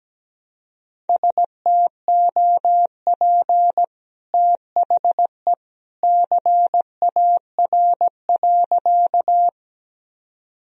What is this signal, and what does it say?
Morse code, heard as a keyed tone, about 17 wpm, 700 Hz. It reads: STOP THE CAR.
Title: Sentences from Top 1000 Words - Review Part 1 - 17wpm